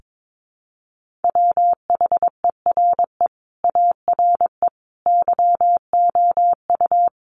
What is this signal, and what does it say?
Morse code, heard as a keyed tone, about 22 wpm, 700 Hz. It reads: WHERE ARE YOU
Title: Sentences from Top 200 Words 22wpm